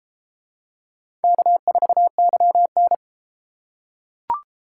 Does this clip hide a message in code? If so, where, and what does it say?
Yes: Morse code from 1.2 to 2.9 s, heard as a keyed tone, about 34 wpm, 700 Hz. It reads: K4YN